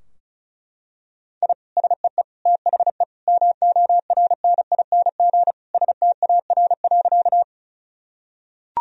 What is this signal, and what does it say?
Morse code, heard as a keyed tone, about 36 wpm, 700 Hz. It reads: I SEE THE MORNING STAR.
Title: Sentences from Top 1000 Words - Review Part 2 - 35wpm